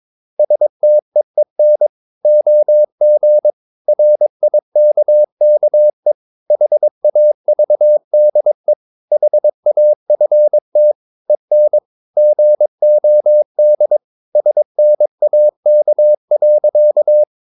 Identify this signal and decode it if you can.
Morse code, heard as a keyed tone, about 22 wpm, 600 Hz. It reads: STEEN OG RIKKE HAVDE HAFT EN GOD SNAK.
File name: Ferie QRS 22wpm